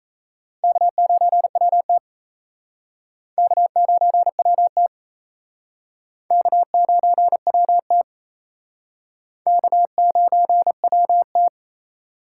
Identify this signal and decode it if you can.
Morse code, heard as a keyed tone, about 34 wpm, 700 Hz. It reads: K9WT K9WT K9WT K9WT
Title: Speed Racing - USA 1x2 Call Signs 28wpm